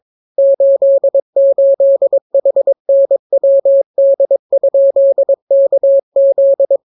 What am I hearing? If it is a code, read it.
Morse code, heard as a keyed tone, about 22 wpm, 550 Hz. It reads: 88HNWD?KZ